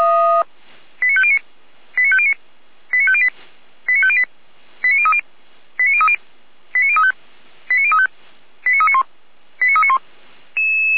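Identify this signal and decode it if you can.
Five-tone selective call (ZVEI), heard as quick runs of five tones, tones of about 70 ms. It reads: 89509 89509 89508 89508 89030 89030 89035 89035 89391 89391
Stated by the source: ZVEI five-tone selective call 89140 received by SDR, logged by feuerwehr-orlamuende fire brigade server.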